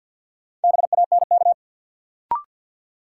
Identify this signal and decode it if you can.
Morse code, heard as a keyed tone, about 53 wpm, 700 Hz. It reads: BANK
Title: Sets of 5 Words - New Word Drill - 50wpm